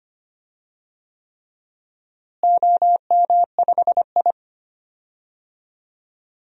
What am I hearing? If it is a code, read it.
Morse code, heard as a keyed tone, about 25 wpm, 700 Hz. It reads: OM5I
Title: DXCC-100 Call Signs - 4 Characters - Repeated 3x - 25wpm